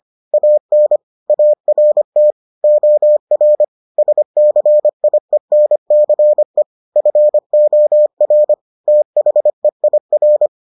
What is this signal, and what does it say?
Morse code, heard as a keyed tone, about 25 wpm, 600 Hz. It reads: AN ART OR SCIENCE FOR THEIR